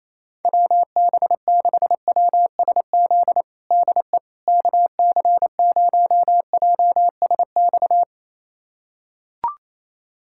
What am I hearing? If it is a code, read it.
Morse code, heard as a keyed tone, about 28 wpm, 700 Hz. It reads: WB6WSZ DE KC0JSX
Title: USA Callsign DE Callsign 28wpm